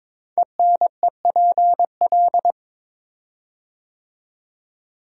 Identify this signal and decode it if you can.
Morse code, heard as a keyed tone, about 22 wpm, 700 Hz. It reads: ENEPL
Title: Code Groups - 5 Letters - 22wpm